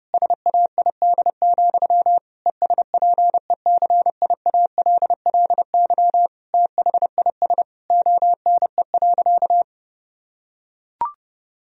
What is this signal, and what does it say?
Morse code, heard as a keyed tone, about 32 wpm, 700 Hz. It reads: SAID, ESPECIALLY THIS ONE.
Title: Sentences from Top 1000 Words - Review Part 4 - 30wpm